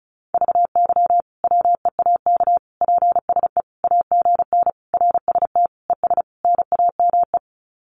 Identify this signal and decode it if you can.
Morse code, heard as a keyed tone, about 34 wpm, 700 Hz. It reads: VY WEAK PSE AGN RST ES NAME
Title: Common QSO Phrases 35wpm